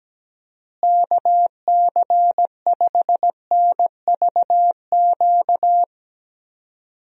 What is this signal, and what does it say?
Morse code, heard as a keyed tone, about 17 wpm, 700 Hz. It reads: KC5NVQ
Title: USA Call Signs - 6 Characters - Repeated 2x - 17wpm